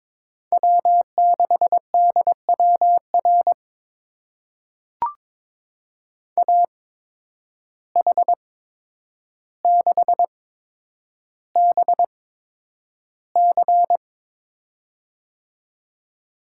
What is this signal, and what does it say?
Morse code, heard as a keyed tone, about 22 wpm, spaced as for 9 wpm, 700 Hz. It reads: W6DWR AH6BC
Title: USA Call Signs - 8x Character Spacing - 22wpm